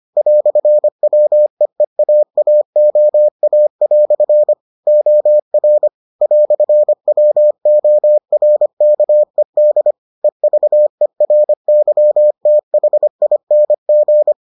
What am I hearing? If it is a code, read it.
Morse code, heard as a keyed tone, about 25 wpm, 600 Hz. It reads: "WEEAAOA" OR "WORKED EVERYTHING